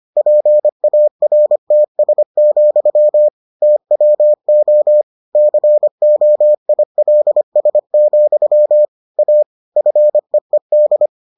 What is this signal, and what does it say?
Morse code, heard as a keyed tone, about 25 wpm, 600 Hz. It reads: PARTS, TWO COILS, A FEED